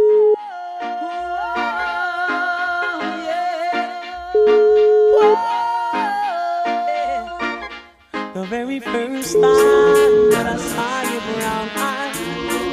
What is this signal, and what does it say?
Ringback tone (Europe, 425 Hz), with noise.